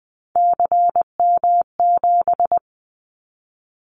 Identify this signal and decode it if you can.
Morse code, heard as a keyed tone, about 20 wpm, 700 Hz. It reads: CM7
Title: Callsign Prefixes for Top 100 DXCC - 20wpm